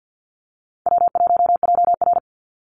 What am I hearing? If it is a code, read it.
Morse code, heard as a keyed tone, about 51 wpm, 700 Hz. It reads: W1JR
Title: Easy Call Signs - 50wpm